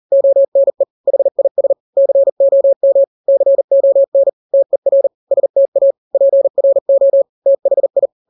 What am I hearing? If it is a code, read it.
Morse code, heard as a keyed tone, about 40 wpm, 550 Hz. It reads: ONE HIS COM CON TER STA PRO THI
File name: n-grams / 3-grams 40 35 WPM